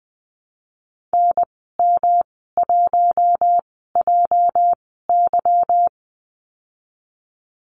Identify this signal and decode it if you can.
Morse code, heard as a keyed tone, about 20 wpm, spaced as for 15 wpm, 700 Hz. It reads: NM1JY
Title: 207 Lesson 33g - Letter j - Call Signs 20wpm